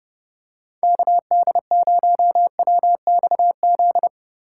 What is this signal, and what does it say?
Morse code, heard as a keyed tone, about 30 wpm, 700 Hz. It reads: KD0WXZ
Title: USA Call Signs - 6 Characters - 4x Character Spacing - 30wpm